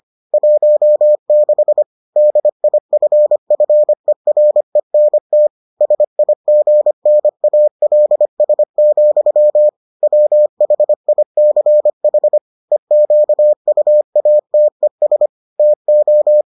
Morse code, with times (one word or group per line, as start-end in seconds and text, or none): 0.3-1.8 16
2.2-5.5 DIFFERENT
5.8-9.7 SIGNALS,
10.0-12.4 WHICH
12.7-15.3 EQUATES
15.6-16.4 TO